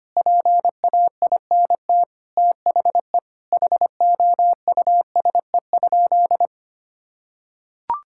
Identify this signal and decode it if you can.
Morse code, heard as a keyed tone, about 25 wpm, 700 Hz. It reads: PAINT THE HOUSE?
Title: Sentences from Top 1000 Words - Review Part 6 - 25wpm